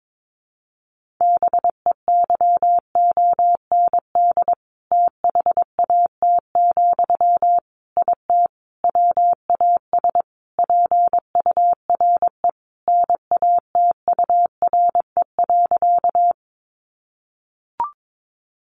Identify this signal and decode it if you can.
Morse code, heard as a keyed tone, about 22 wpm, 700 Hz. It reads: BEYOND THAT, IT WAS PURE NATURE.
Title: Sentences from Top 1500 Words 22wpm